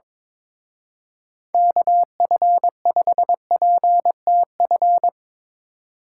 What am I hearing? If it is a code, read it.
Morse code, heard as a keyed tone, about 24 wpm, 700 Hz. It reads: KF5PTF
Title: Speed Racing - USA 2x3 Call Signs 22wpm